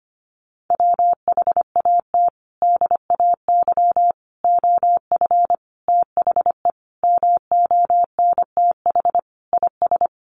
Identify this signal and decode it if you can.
Morse code, heard as a keyed tone, about 25 wpm, 700 Hz. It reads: WHAT DAY OF THE MONTH IS
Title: Sentences from Top 800 Words 25wpm